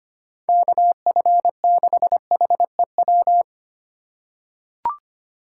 Morse code, heard as a keyed tone, about 25 wpm, 700 Hz. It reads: KF6HEW
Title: USA Call Signs - 6 Characters - 2x Character Spacing - 25wpm